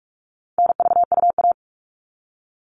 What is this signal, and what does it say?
Morse code, heard as a keyed tone, about 44 wpm, 700 Hz. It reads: NVUA
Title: All Cut Numbers - 4-Digit 45wpm